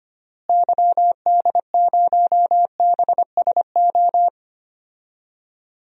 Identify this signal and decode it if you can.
Morse code, heard as a keyed tone, about 25 wpm, 700 Hz. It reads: YD0BSO
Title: DXCC-100 Call Signs - 6 Characters - Repeated 3x - 25wpm